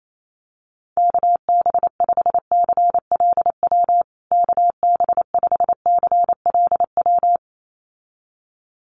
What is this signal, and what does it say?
Morse code, heard as a keyed tone, about 28 wpm, 700 Hz. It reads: KB5CLW KB5CLW
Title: USA Call Signs Repeated 2x 28wpm